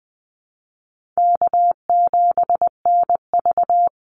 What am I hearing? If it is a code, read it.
Morse code, heard as a keyed tone, about 20 wpm, 700 Hz. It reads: K7NV
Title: Call Signs - 20wpm